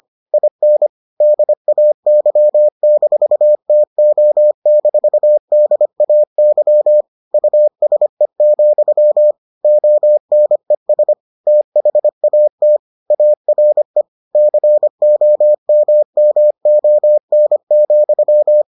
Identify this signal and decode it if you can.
Morse code, heard as a keyed tone, about 25 wpm, 600 Hz. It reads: IN DAY-TO-DAY USE, ONES THAT ARE COMMON,